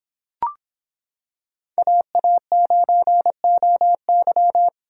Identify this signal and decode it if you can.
Morse code, heard as a keyed tone, about 26 wpm, 700 Hz. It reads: AA9OY